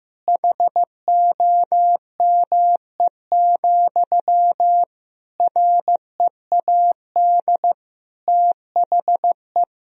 Morse code, heard as a keyed tone, about 15 wpm, 700 Hz. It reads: HOME, READ THE
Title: Sentences from Top 1000 Words - Review Part 2 - 15wpm